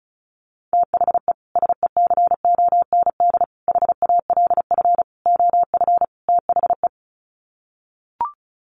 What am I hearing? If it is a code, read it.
Morse code, heard as a keyed tone, about 34 wpm, 700 Hz. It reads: THE SECOND HALF OF THE
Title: Sets of 5 Words 35wpm